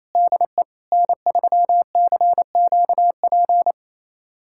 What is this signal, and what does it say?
Morse code, heard as a keyed tone, about 28 wpm, 700 Hz. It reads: DE N3CQP
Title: USA Callsign DE Callsign - 5 Characters - 28wpm